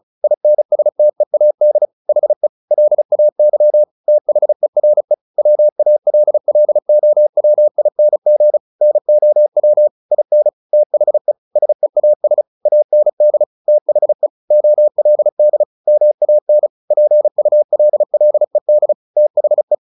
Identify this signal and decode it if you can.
Morse code, heard as a keyed tone, about 34 wpm, 600 Hz. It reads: INSTEAD HE LAY THERE WALLOWING NOW IN THE SEAS AND THE OLD MAN PULLED THE
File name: the old man and the sea 2 35wpm